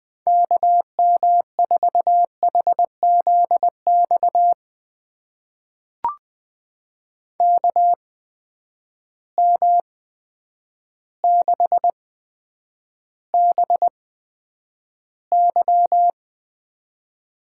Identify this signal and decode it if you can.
Morse code, heard as a keyed tone, about 20 wpm, spaced as for 9 wpm, 700 Hz. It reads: KM4HZX KM6BY
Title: USA Call Signs - 6 Characters - 8x Character Spacing - 20wpm